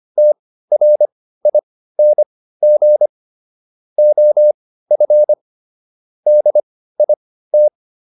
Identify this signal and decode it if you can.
Morse code, heard as a keyed tone, about 25 wpm, spaced as for 15 wpm, 600 Hz. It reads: TRING OF DIT